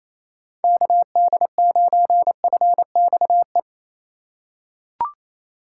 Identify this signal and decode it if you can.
Morse code, heard as a keyed tone, about 27 wpm, 700 Hz. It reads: KD9FXE
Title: USA Call Signs - 4x Character Spacing - 28wpm